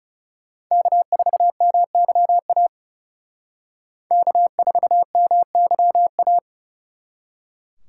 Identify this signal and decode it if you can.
Morse code, heard as a keyed tone, about 38 wpm, 700 Hz. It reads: K4MYA K4MYA